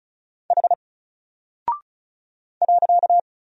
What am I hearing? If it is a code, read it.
Morse code, heard as a keyed tone, about 34 wpm, 700 Hz. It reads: H .
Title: Single Character (Letter:Number:Punctuation:Prosign) 35wpm